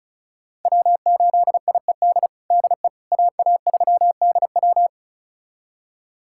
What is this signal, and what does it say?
Morse code, heard as a keyed tone, about 36 wpm, 700 Hz. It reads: W8IED DE AA3DW
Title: USA Callsign DE Callsign - 5 Characters - 35wpm